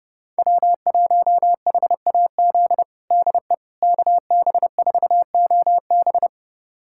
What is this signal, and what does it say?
Morse code, heard as a keyed tone, about 30 wpm, 700 Hz. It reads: W1HAZ DE KB4OB
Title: USA Callsign DE Callsign - 5 Characters - 30wpm